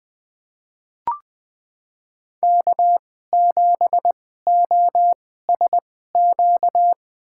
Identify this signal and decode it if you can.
Morse code, heard as a keyed tone, about 20 wpm, spaced as for 14 wpm, 700 Hz. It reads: K7OSQ
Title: USA Call Signs - 5 Characters - 2x Character Spacing - 20wpm